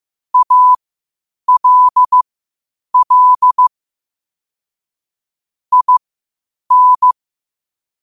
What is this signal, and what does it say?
Morse code, heard as a keyed tone, about 15 wpm, spaced as for 8 wpm, 1000 Hz. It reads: ALL IN
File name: qotdfiles / 8wpm